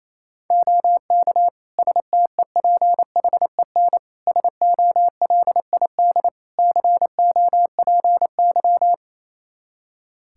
Morse code, heard as a keyed tone, about 27 wpm, 700 Hz. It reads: OK STEPHEN SOLID COPY
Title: K1USN SST QSO Practice 28wpm